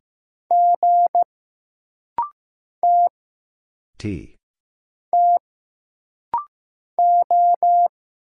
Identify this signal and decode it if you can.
Morse code, heard as a keyed tone, about 15 wpm, 700 Hz. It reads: G T T O
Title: Single Letter-Number Alt - Rapid-Fire 15wpm